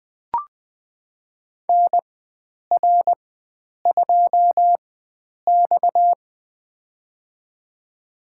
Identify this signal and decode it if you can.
Morse code, heard as a keyed tone, about 20 wpm, spaced as for 9 wpm, 700 Hz. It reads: NR2X
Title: USA Call Signs - 4 Characters - 4x Character Spacing - 20wpm